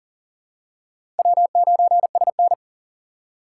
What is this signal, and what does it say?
Morse code, heard as a keyed tone, about 40 wpm, 700 Hz. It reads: W9SN
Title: USA Call Signs - 4 Characters - 6x Character Spacing - 40wpm